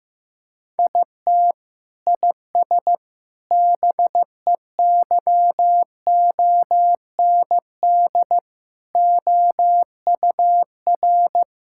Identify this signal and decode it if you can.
Morse code, heard as a keyed tone, about 15 wpm, 700 Hz. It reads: IT IS BEYOND OUR
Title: Sentences from Top 1000 Words - Review Part 6 - 15wpm